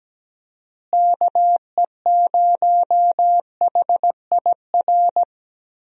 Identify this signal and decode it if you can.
Morse code, heard as a keyed tone, about 17 wpm, 700 Hz. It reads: KE0HIR
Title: USA Call Signs - 6 Characters - Repeated 3x - 17wpm